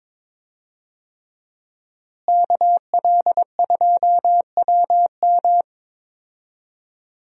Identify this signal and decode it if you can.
Morse code, heard as a keyed tone, about 21 wpm, 700 Hz. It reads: KL2WM
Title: USA Call Signs - 5 Characters - Repeated 3x - 22wpm